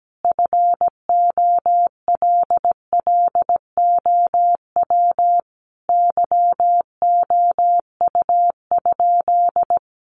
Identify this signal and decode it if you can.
Morse code, heard as a keyed tone, about 17 wpm, 700 Hz. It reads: FOLLOW YOU?
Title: Sentences from Top 200 Words 17wpm